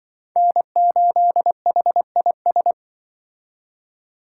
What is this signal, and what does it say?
Morse code, heard as a keyed tone, about 24 wpm, 700 Hz. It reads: N8HIS
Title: Speed Racing - USA 1x3 Call Signs 20wpm